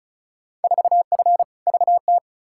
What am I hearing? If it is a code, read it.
Morse code, heard as a keyed tone, about 36 wpm, 700 Hz. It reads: 4F VT
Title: Field Day Exchanges Encore - 35wpm